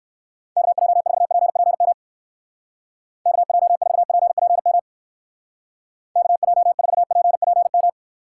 Morse code, heard as a keyed tone, about 64 wpm, 700 Hz. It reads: KJ4PPM KJ4PPM KJ4PPM